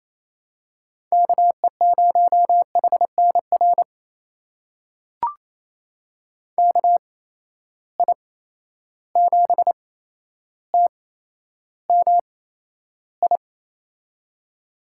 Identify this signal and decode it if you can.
Morse code, heard as a keyed tone, about 28 wpm, spaced as for 12 wpm, 700 Hz. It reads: KE0HNR KI7TMI
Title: USA Call Signs - 6 Characters - 8x Character Spacing - 28wpm